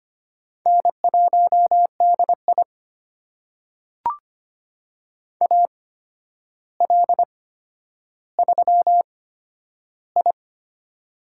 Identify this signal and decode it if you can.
Morse code, heard as a keyed tone, about 25 wpm, spaced as for 11 wpm, 700 Hz. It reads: N1DI AL3I
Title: USA Call Signs - 4 Characters - 8x Character Spacing - 25wpm